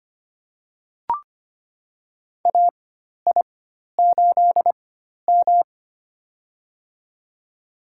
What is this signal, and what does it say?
Morse code, heard as a keyed tone, about 25 wpm, spaced as for 12 wpm, 700 Hz. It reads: AI8M
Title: USA Call Signs - 4 Characters - 4x Character Spacing - 25wpm